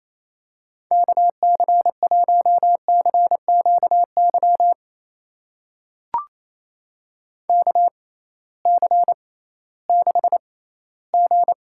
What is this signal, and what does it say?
Morse code, heard as a keyed tone, about 27 wpm, spaced as for 16 wpm, 700 Hz. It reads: KC1CQY KC6G
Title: USA Call Signs - 6x Character Spacing - 28wpm